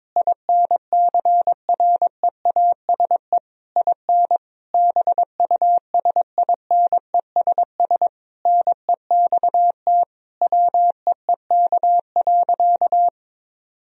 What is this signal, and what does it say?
Morse code, heard as a keyed tone, about 22 wpm, 700 Hz. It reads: INCREASE IN BUSINESS NEXT WEEK.
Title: Sentences from Top 1000 Words - Review Part 4 - 22wpm